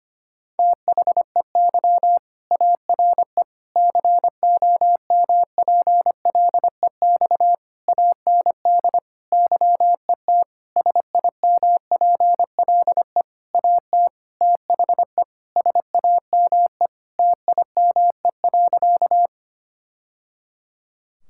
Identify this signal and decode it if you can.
Morse code, heard as a keyed tone, about 25 wpm, 700 Hz. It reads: THEY ARE COMPLEX AND YET SIMPLE AT THE SAME TIME.
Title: Sentences from Top 1000 Words - Review Part 6 - 25wpm